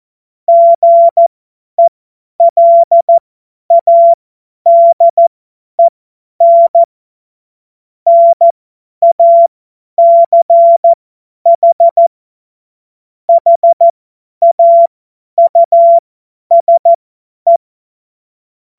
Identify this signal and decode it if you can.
Morse code, heard as a keyed tone, about 14 wpm, spaced as for 10 wpm, 680 Hz. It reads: GELADEN NACH HAUSE
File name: Morsekurs / Witze / tz 08. November 2022 50 F70